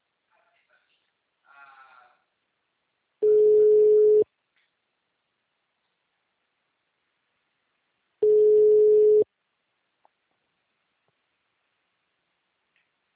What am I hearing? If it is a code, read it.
Ringback tone (Europe, 425 Hz), with speech.